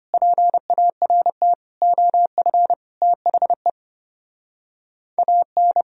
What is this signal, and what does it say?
Morse code, heard as a keyed tone, about 33 wpm, 700 Hz. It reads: PART OF THE AN